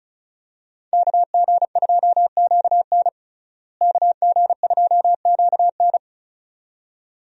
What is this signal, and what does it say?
Morse code, heard as a keyed tone, about 36 wpm, 700 Hz. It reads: KG2QN KG2QN
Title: USA Call Signs - 5 Characters - Repeated 2x - 35wpm